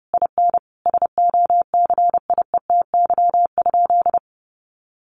Morse code, heard as a keyed tone, about 30 wpm, 700 Hz. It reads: IN SOCIETY?
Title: Sentences from Top 700 Words 30wpm